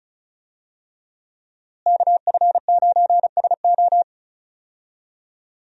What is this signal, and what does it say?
Morse code, heard as a keyed tone, about 34 wpm, 700 Hz. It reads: KF9SO
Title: USA Call Signs - 5 Characters - Repeated 3x - 35wpm